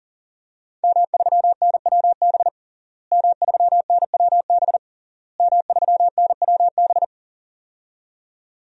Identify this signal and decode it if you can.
Morse code, heard as a keyed tone, about 40 wpm, 700 Hz. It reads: M3NWB M3NWB M3NWB
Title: DXCC-100 Call Signs - 5 Characters - Repeated 3x - 40wpm